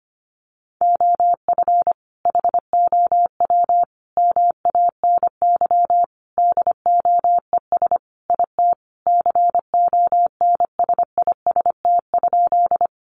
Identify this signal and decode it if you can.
Morse code, heard as a keyed tone, about 25 wpm, 700 Hz. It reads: OF HOW MANY DOES IT CONSIST?
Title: Sentences from Top 1400 Words 25wpm